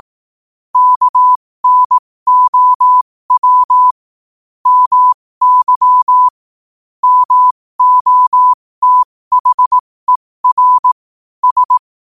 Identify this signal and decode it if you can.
Morse code, heard as a keyed tone, about 18 wpm, spaced as for 15 wpm, 1000 Hz. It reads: KNOW MY MOTHER S